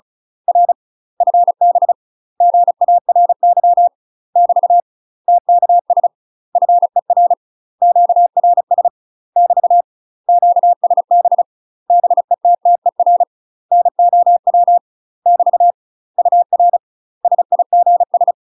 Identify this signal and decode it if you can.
Morse code, heard as a keyed tone, about 36 wpm, 700 Hz. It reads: R FB GARY <BT> TKS FER QRS <BT> QSB BETTER NOW <BT> UR SIGS
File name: UK MASTER QSOs TRAINING / 35WPM / UK MASTER QSOs 0012